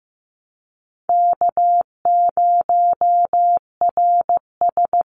Morse code, heard as a keyed tone, about 15 wpm, 700 Hz. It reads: K0RS
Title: NA SOTA Chaser Callsigns 15wpm